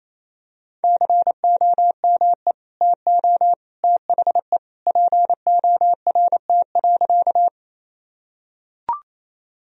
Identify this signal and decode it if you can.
Morse code, heard as a keyed tone, about 28 wpm, 700 Hz. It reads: COME TO THE PORT.
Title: Sentences from Top 1000 Words - Review Part 1 - 28wpm